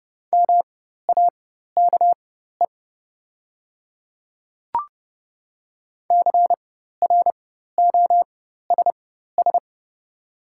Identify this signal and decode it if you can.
Morse code, heard as a keyed tone, about 30 wpm, spaced as for 14 wpm, 700 Hz. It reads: MAKE CROSS
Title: 4x Character Spacing - Top 300 Words - 30wpm